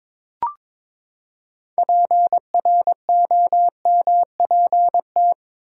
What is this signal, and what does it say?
Morse code, heard as a keyed tone, about 22 wpm, 700 Hz. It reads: PROMPT
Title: Six Letter Words - 22wpm